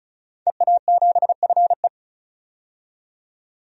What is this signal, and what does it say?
Morse code, heard as a keyed tone, about 36 wpm, 700 Hz. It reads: EA7FE
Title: DXCC-100 Call Signs - 5 Characters - 35wpm